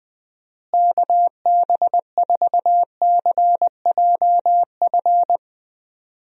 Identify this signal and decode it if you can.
Morse code, heard as a keyed tone, about 20 wpm, 700 Hz. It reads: KB4CJF
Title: USA Call Signs - 6 Characters - Repeated 2x - 20wpm